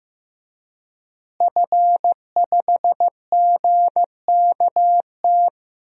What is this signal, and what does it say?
Morse code, heard as a keyed tone, about 15 wpm, 700 Hz. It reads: F5GKT
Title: DXCC-100 Call Signs - 5 Characters - Repeated 3x - 15wpm